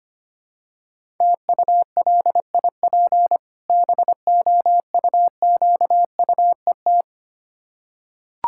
Morse code, heard as a keyed tone, about 25 wpm, 690 Hz. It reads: TULIP BOUQUET
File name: practice / 25wpm / Difficult Words with Hints - 25wpm